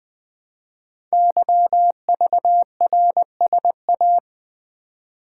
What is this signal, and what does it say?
Morse code, heard as a keyed tone, about 20 wpm, 700 Hz. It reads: YVRSA
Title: Code Groups - 5 Letters - 20wpm